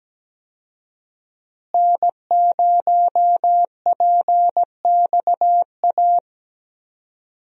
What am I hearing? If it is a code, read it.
Morse code, heard as a keyed tone, about 17 wpm, 700 Hz. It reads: N0PXA
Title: USA Call Signs - 5 Characters - Repeated 3x - 17wpm